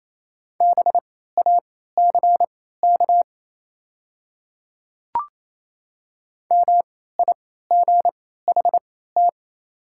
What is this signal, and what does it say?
Morse code, heard as a keyed tone, about 28 wpm, spaced as for 16 wpm, 690 Hz. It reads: BACK MIGHT
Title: 3x Character Spacing - Top 500 Words - 28wpm